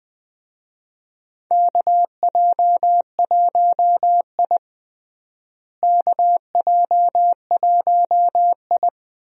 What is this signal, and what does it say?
Morse code, heard as a keyed tone, about 21 wpm, 700 Hz. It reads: KJ1I KJ1I
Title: USA Call Signs - 4 Characters - Repeated 3x - 20wpm